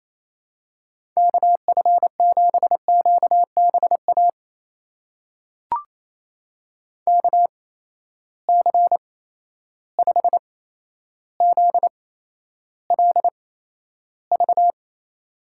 Morse code, heard as a keyed tone, about 28 wpm, spaced as for 12 wpm, 700 Hz. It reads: KF7QBA KC5ZLV
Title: USA Call Signs - 6 Characters - 8x Character Spacing - 28wpm